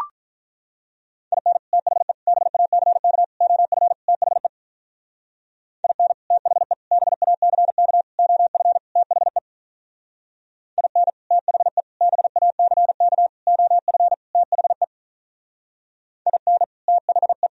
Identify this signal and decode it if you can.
Morse code, heard as a keyed tone, about 43 wpm, 700 Hz. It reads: IN THE BACK OF THE IN THE BACK OF THE IN THE BACK OF THE IN THE
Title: Speed-Racing - Sets of 5 Words 35wpm